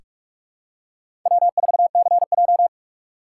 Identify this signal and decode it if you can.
Morse code, heard as a keyed tone, about 46 wpm, 700 Hz. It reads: W4CJ
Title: USA Call Signs - 4 Characters - 2x Character Spacing - 45wpm